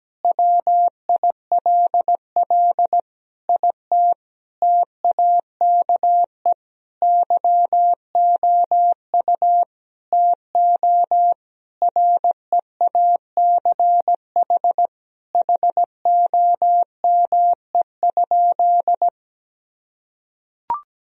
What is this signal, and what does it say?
Morse code, heard as a keyed tone, about 17 wpm, 700 Hz. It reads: WILL IT TAKE YOU TO REACH HOME?
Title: Sentences from Top 1000 Words - Review Part 2 - 17wpm